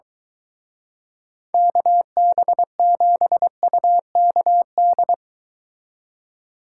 Morse code, heard as a keyed tone, about 23 wpm, 700 Hz. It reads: KB7UKD